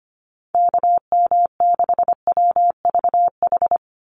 Morse code, heard as a keyed tone, about 25 wpm, 700 Hz. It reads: KM6WVH
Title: USA Call Signs 25wpm